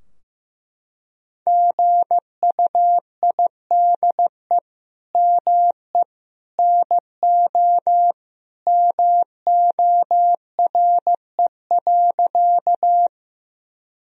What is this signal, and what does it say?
Morse code, heard as a keyed tone, about 15 wpm, 700 Hz. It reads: GUIDE ME NO MORE.
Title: Sentences from Top 1000 Words - Review Part 3 - 15wpm